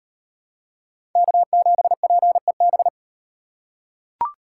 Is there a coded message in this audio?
Yes: Morse code from 1.2 to 2.9 s, heard as a keyed tone, about 38 wpm, 700 Hz. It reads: K7PEB